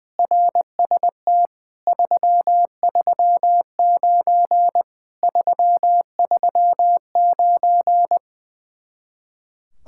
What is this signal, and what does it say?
Morse code, heard as a keyed tone, about 20 wpm, 700 Hz. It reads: RST 339 339
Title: Speed-Racing Common QSO Phrases - 20wpm